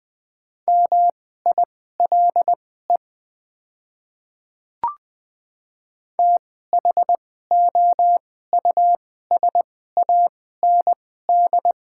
Morse code, heard as a keyed tone, about 20 wpm, spaced as for 14 wpm, 690 Hz. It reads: MILE THOUSAND
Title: 2x Character Spacing - Top 500 Words - 20wpm